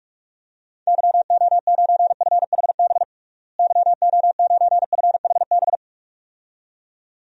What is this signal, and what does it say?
Morse code, heard as a keyed tone, about 47 wpm, 700 Hz. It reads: YO9FHB YO9FHB
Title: DXCC-100 Call Signs - 6 Characters - Repeated 2x - 45wpm